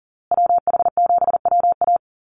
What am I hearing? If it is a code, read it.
Morse code, heard as a keyed tone, about 40 wpm, 700 Hz. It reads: WH7WA
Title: USA Call Signs Repeated 2x 40wpm